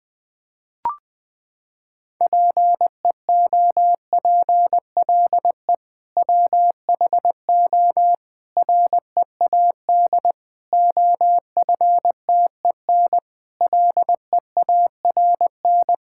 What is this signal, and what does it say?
Morse code, heard as a keyed tone, about 20 wpm, 700 Hz. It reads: PEOPLE WHO READ OFTEN LEARN
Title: Sentences with 2-Word Phrases - 20wpm